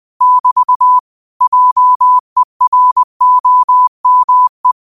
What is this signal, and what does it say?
Morse code, heard as a keyed tone, about 20 wpm, 1000 Hz. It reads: <BT> JEROME
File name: qotdfiles / 20wpm